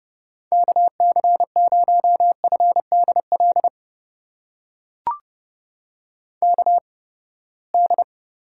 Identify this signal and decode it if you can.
Morse code, heard as a keyed tone, about 29 wpm, spaced as for 21 wpm, 700 Hz. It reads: KC0FDL KD
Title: USA Call Signs - 6 Characters - 8x Character Spacing - 30wpm